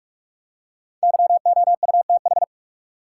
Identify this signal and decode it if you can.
Morse code, heard as a keyed tone, about 46 wpm, 700 Hz. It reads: YOUTH
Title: Five Letter Words - 45wpm